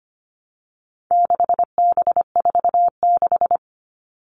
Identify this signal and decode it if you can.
Morse code, heard as a keyed tone, about 25 wpm, 700 Hz. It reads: 6B46
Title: All Cut Numbers - 4-Digit 25wpm